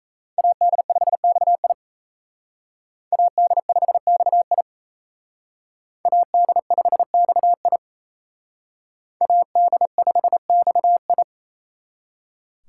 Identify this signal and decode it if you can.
Morse code, heard as a keyed tone, about 36 wpm, 700 Hz. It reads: AD5XI AD5XI AD5XI AD5XI